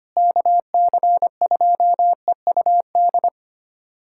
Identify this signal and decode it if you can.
Morse code, heard as a keyed tone, about 25 wpm, 700 Hz. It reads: KC2EUD
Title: USA Callsign DE Callsign 25wpm